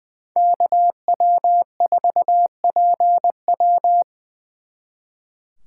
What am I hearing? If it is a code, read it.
Morse code, heard as a keyed tone, about 25 wpm, 700 Hz. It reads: KW4PW
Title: Speed Racing - USA 2x2 Call Signs 20wpm